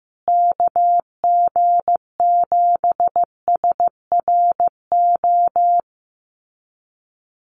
Morse code, heard as a keyed tone, about 15 wpm, 700 Hz. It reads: KG7SRO
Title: USA Call Signs Repeated 3x 15wpm